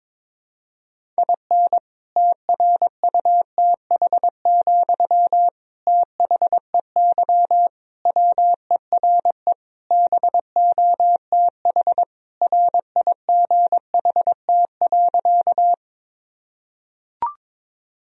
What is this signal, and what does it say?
Morse code, heard as a keyed tone, about 21 wpm, 700 Hz. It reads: IN TRUTH, THEY WERE BOTH RIGHT.
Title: Sentences from Top 1000 Words - Review Part 1 - 22wpm